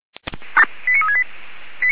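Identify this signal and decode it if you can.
Five-tone selective call (ZVEI), heard as quick runs of five tones, tones of about 80 ms. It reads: 89368
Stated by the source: ZVEI five-tone selective call received by SDR.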